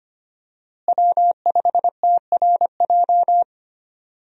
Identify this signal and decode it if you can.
Morse code, heard as a keyed tone, about 25 wpm, 700 Hz. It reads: W5TRJ
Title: USA Call Signs - 5 Characters - Repeated 3x - 25wpm